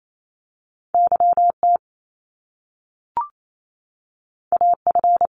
Morse code, heard as a keyed tone, about 28 wpm, 700 Hz. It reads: YT AF
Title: Top Two Letter Combinations in Words 28wpm